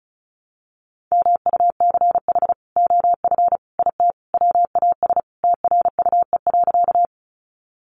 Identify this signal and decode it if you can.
Morse code, heard as a keyed tone, about 34 wpm, 700 Hz. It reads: MUCH OF IT WAS TRUE.
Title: Sentences from Top 400 Words 35wpm